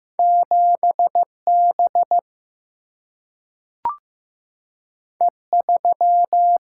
Morse code, heard as a keyed tone, about 15 wpm, 700 Hz. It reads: 7 B E 3
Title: Single-Letter Single-Number 15wpm